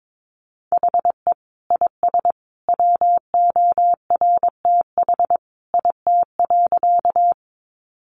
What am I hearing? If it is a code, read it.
Morse code, heard as a keyed tone, about 22 wpm, 700 Hz. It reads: HE IS WORTH IT.
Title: Sentences from Top 800 Words 22wpm